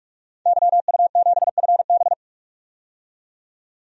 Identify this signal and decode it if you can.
Morse code, heard as a keyed tone, about 44 wpm, 700 Hz. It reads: YU7FB
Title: DXCC-100 Call Signs - 5 Characters - 45wpm